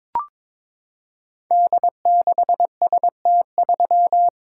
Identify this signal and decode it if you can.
Morse code, heard as a keyed tone, about 22 wpm, 700 Hz. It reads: D6ST3
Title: Code Groups - 5 Letter:Numbers - 22wpm